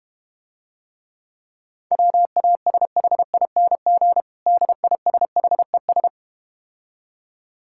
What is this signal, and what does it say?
Morse code, heard as a keyed tone, about 32 wpm, 700 Hz. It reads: WASHING DISHES